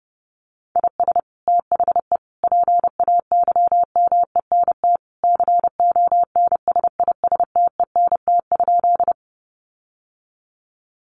Morse code, heard as a keyed tone, about 30 wpm, 700 Hz. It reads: IS THE PAYMENT CONSISTENT?
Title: Sentences from Top 1500 Words 30wpm